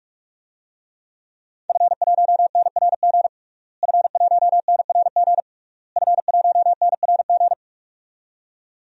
Morse code, heard as a keyed tone, about 46 wpm, 700 Hz. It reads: F1NRG F1NRG F1NRG
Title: DXCC-100 Call Signs - 5 Characters - Repeated 3x - 45wpm